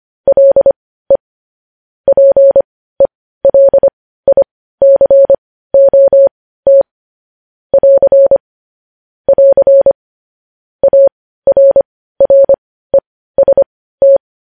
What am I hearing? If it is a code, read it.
Morse code, heard as a keyed tone, about 25 wpm, spaced as for 15 wpm, 550 Hz. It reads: LE PELICOT <AR> <AR> ARREST
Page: news headlines in morse code at 15 wpm